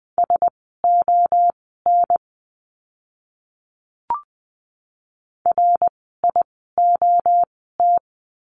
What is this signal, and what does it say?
Morse code, heard as a keyed tone, about 20 wpm, spaced as for 14 wpm, 700 Hz. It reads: SON RIOT
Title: 061 Lesson 11f - Letter h - Words 20wpm